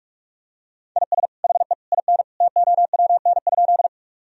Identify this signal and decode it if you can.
Morse code, heard as a keyed tone, about 41 wpm, 700 Hz. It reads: IS HE IN TOWN?
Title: Sentences from Top 1000 Words - Review Part 2 - 45wpm